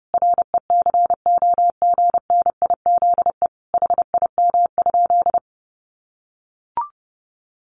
Morse code, heard as a keyed tone, about 30 wpm, 700 Hz. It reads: RECOGNIZE HIM?
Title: Sentences from Top 1400 Words 30wpm